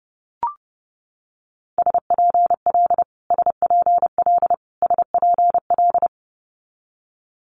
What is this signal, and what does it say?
Morse code, heard as a keyed tone, about 30 wpm, 700 Hz. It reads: SPL SPL SPL
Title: Multi-letter Phonemes Repeated 3x 30wpm